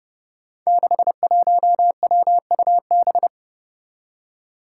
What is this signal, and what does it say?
Morse code, heard as a keyed tone, about 32 wpm, 700 Hz. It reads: 61WUB
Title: Five Challenging Letter-Number Combinations 30wpm